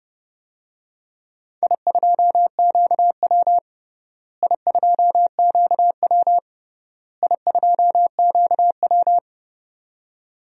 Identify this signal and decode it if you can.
Morse code, heard as a keyed tone, about 30 wpm, 700 Hz. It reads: I2QW I2QW I2QW
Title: DXCC-100 Call Signs - 4 Characters - Repeated 3x - 30wpm